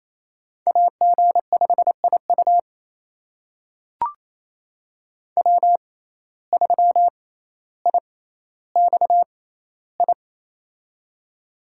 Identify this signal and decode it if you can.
Morse code, heard as a keyed tone, about 28 wpm, spaced as for 14 wpm, 700 Hz. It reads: AG5IU W3IXI
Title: USA Call Signs - 5 Characters - 6x Character Spacing - 28wpm